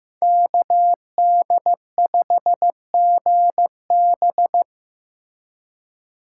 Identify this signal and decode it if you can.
Morse code, heard as a keyed tone, about 18 wpm, 700 Hz. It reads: KD5GB
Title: Speed Racing - USA Call Signs 15wpm